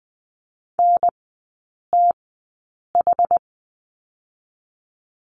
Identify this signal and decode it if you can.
Morse code, heard as a keyed tone, about 20 wpm, spaced as for 8 wpm, 700 Hz. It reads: NTH
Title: 060 Lesson 11e - Letter h - Three Characters 20wpm